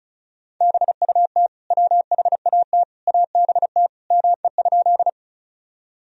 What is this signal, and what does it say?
Morse code, heard as a keyed tone, about 36 wpm, 700 Hz. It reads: BUT WHAT ABT ME?
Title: Sentences from Top 200 Words and Abbreviations 35wpm